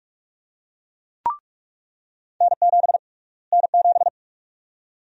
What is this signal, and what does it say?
Morse code, heard as a keyed tone, about 41 wpm, 700 Hz. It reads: N7 N7
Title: USA Call Sign Prefixes - Repeated 2x - 45wpm